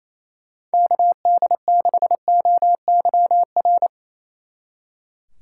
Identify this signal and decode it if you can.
Morse code, heard as a keyed tone, about 35 wpm, 700 Hz. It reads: KD6OYR
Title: Speed Racing - USA 2x3 Call Signs 28wpm